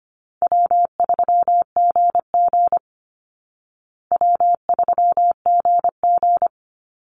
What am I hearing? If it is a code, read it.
Morse code, heard as a keyed tone, about 25 wpm, 700 Hz. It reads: W3GG W3GG
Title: Easy Call Signs Repeated 3x - 25wpm